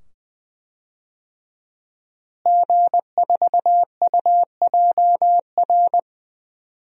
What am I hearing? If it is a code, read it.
Morse code, heard as a keyed tone, about 20 wpm, 700 Hz. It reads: G4UJR
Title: DXCC-100 Call Signs - 5 Characters - Repeated 2x - 20wpm